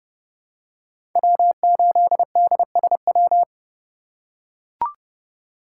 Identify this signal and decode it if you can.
Morse code, heard as a keyed tone, about 30 wpm, 700 Hz. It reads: W8DSW
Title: USA Call Signs - 5 Characters - 2x Character Spacing - 30wpm